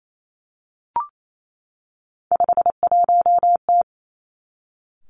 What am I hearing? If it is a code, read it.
Morse code, heard as a keyed tone, about 28 wpm, 700 Hz. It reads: 51T
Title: Three Challenging Letter-Number Combinations 28wpm